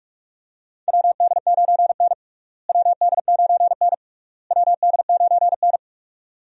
Morse code, heard as a keyed tone, about 46 wpm, 700 Hz. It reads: WD9N WD9N WD9N
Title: USA Call Signs - 4 Characters - Repeated 3x - 45wpm